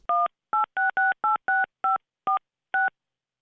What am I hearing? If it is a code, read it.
DTMF, heard as telephone touch tones, tones of about 160 ms. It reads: 186686546